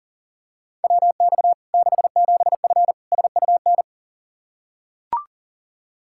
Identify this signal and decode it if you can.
Morse code, heard as a keyed tone, about 40 wpm, 700 Hz. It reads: WX 67F SUN
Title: K1USN SST QSO Practice 40wpm